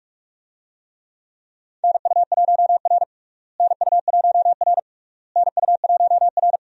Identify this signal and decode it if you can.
Morse code, heard as a keyed tone, about 46 wpm, 700 Hz. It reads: NU1R NU1R NU1R
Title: USA Call Signs - 4 Characters - Repeated 3x - 45wpm